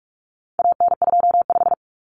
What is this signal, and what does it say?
Morse code, heard as a keyed tone, about 44 wpm, 700 Hz. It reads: AN25
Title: Common Cut Numbers - 4-Digit 45wpm